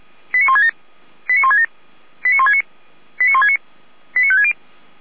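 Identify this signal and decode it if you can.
Five-tone selective call (ZVEI), heard as quick runs of five tones, tones of about 80 ms. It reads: 89167 89167 89169 89169 89560
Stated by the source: ZVEI five-tone selective call received via SDR, from the Feuerwehr Orlamünde audio server.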